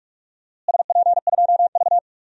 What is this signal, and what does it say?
Morse code, heard as a keyed tone, about 46 wpm, 690 Hz. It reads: SP2V
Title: DXCC-100 Call Signs - 4 Characters - Repeated 3x - 45wpm